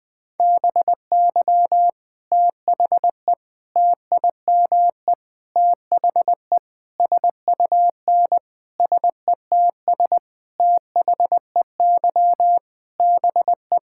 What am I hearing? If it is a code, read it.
Morse code, heard as a keyed tone, about 20 wpm, 700 Hz. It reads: BY THE TIME THE SUN SETS THEY BE
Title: Sentences with 4-Word Phrases - 20wpm